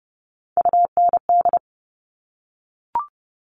Morse code, heard as a keyed tone, about 30 wpm, 700 Hz. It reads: UND
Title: Top Three Letter Combinations in Words 30wpm